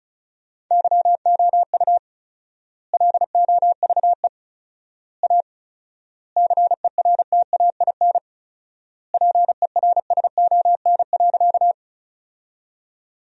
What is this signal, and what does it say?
Morse code, heard as a keyed tone, about 36 wpm, 700 Hz. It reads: YOU LOVE A CERTAIN PERSON.
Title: 4x Word-Spacing - Sentences from Top 500 Words - 35wpm